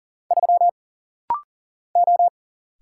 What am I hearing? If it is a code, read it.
Morse code, heard as a keyed tone, about 40 wpm, 700 Hz. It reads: 3 O
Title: Single Letter-Number Alt - Rapid-Fire 40wpm